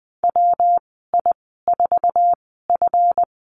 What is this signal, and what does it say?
Morse code, heard as a keyed tone, about 20 wpm, spaced as for 15 wpm, 700 Hz. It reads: WI4F
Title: 154 Lesson 25g - Letter p - Call Signs 20wpm